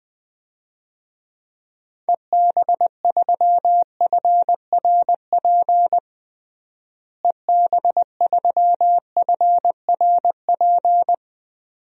Morse code, heard as a keyed tone, about 21 wpm, 690 Hz. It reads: EB3FRP EB3FRP
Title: DXCC-100 Call Signs - 6 Characters - Repeated 2x - 20wpm